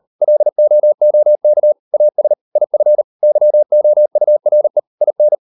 Morse code, heard as a keyed tone, about 40 wpm, 600 Hz. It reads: LOOK AS IF YOURE IN